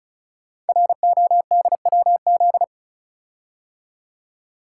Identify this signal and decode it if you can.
Morse code, heard as a keyed tone, about 36 wpm, 700 Hz. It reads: RODWZ
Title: Code Groups - 5 Letters - 35wpm